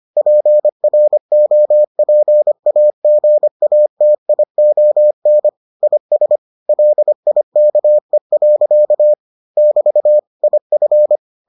Morse code, heard as a keyed tone, about 25 wpm, 600 Hz. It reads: PROPAGATION IS LIKE. <BT> IF